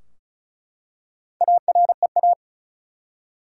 Morse code, heard as a keyed tone, about 36 wpm, 700 Hz. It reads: AREA
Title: Four Letter Words - 35wpm